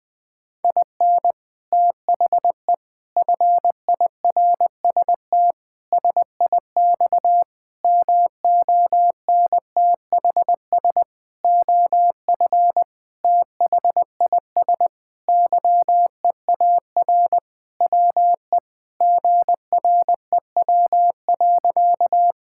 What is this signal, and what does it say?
Morse code, heard as a keyed tone, about 20 wpm, 700 Hz. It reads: IN THE FIRST SIX MONTHS OF THIS YEAR WE GREW.
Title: Sentences with 8-Word Phrases - 20wpm